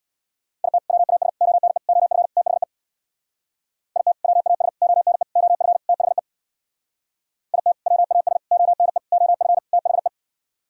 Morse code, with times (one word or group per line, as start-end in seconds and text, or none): 0.6-0.8 IT
0.9-1.3 WAS
1.4-1.8 ONE
1.9-2.3 OF
2.4-2.6 THE
4.0-4.1 IT
4.2-4.7 WAS
4.8-5.2 ONE
5.4-5.8 OF
5.9-6.2 THE
7.5-7.7 IT
7.9-8.4 WAS
8.5-9.0 ONE
9.1-9.6 OF
9.7-10.1 THE